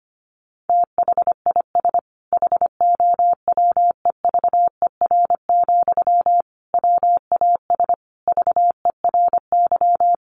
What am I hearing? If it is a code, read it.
Morse code, heard as a keyed tone, about 25 wpm, 700 Hz. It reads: THIS HOWEVER, WAS VERY
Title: Sentences from Top 1400 Words 25wpm